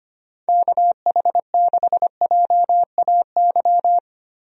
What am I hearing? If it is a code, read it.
Morse code, heard as a keyed tone, about 25 wpm, 700 Hz. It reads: KH6JAY
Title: DXCC-100 Call Signs - 6 Characters - 25wpm